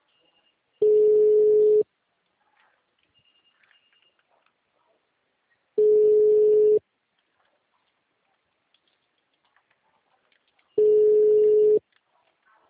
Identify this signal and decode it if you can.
Ringback tone (Europe, 425 Hz).